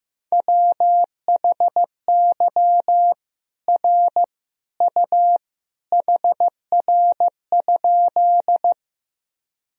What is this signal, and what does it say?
Morse code, heard as a keyed tone, about 15 wpm, 700 Hz. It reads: WHY R U HR?
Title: Sentences from Top 200 Words and Abbreviations 15wpm